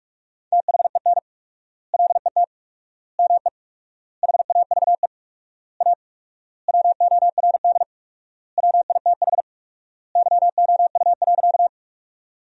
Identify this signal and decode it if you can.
Morse code, heard as a keyed tone, about 46 wpm, 700 Hz. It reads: THEN LET ME HAVE A WORD WITH YOU.
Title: 4x Word-Spacing - Sentences from Top 300 Words - 45wpm